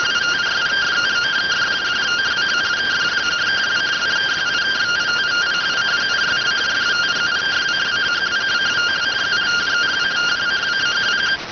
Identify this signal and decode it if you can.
Two-tone teleprinter signal, about 170 Hz shift.